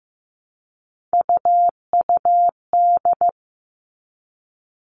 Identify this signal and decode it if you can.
Morse code, heard as a keyed tone, about 15 wpm, 700 Hz. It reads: UUD
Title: All Cut Numbers - 3-Digit 15wpm